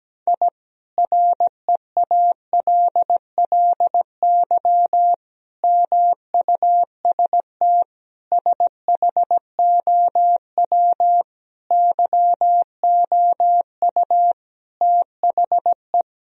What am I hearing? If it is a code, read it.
Morse code, heard as a keyed tone, about 17 wpm, 700 Hz. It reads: I REALLY MUST SHOW YOU THE
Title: Sentences from Top 1000 Words - Review Part 4 - 17wpm